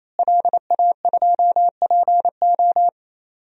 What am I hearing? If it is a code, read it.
Morse code, heard as a keyed tone, about 27 wpm, 700 Hz. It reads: LA2PO
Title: DXCC-100 Call Signs - 5 Characters - Repeated 3x - 28wpm